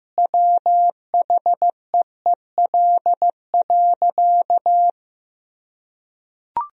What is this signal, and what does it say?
Morse code, heard as a keyed tone, about 15 wpm, 700 Hz. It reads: WHEEL.
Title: Sentences from Top 1000 Words - Review Part 3 - 15wpm